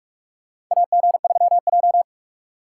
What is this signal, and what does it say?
Morse code, heard as a keyed tone, about 43 wpm, 700 Hz. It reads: AG3J
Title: USA Call Signs - 4 Characters - 6x Character Spacing - 45wpm